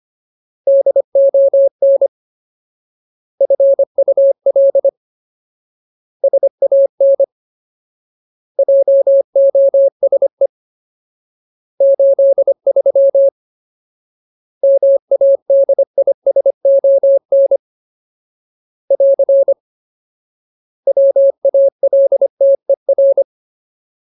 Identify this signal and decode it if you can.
Morse code, heard as a keyed tone, about 25 wpm, 550 Hz. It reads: DON FUL SAN JOSE 83 MADISON <AR> WALTER
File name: Ultimate Word List / 25WPM Ultimate